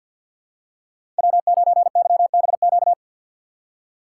Morse code, heard as a keyed tone, about 46 wpm, 690 Hz. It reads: W9YBQ
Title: USA Call Signs - 5 Characters - 3x Character Spacing - 50wpm